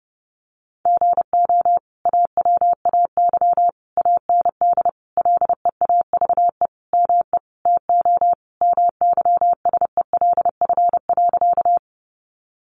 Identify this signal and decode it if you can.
Morse code, heard as a keyed tone, about 30 wpm, 700 Hz. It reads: GO AWAY AND LEAVE ME TO MYSELF.
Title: Sentences from Top 1300 Words 30wpm